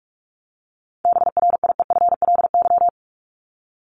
Morse code, heard as a keyed tone, about 44 wpm, 700 Hz. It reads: BRIEFLY
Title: LY Suffix - Top 100 Words with Base Words 45wpm